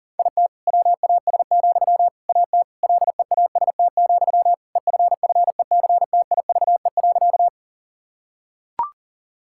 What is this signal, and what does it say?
Morse code, heard as a keyed tone, about 40 wpm, 700 Hz. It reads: IT WAS, AT LEAST, EFFECTIVE.
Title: Sentences from Top 1000 Words - Review Part 6 - 40wpm